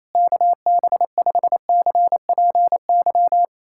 Morse code, heard as a keyed tone, about 28 wpm, 700 Hz. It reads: KB5CPY
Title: USA Callsign DE Callsign 28wpm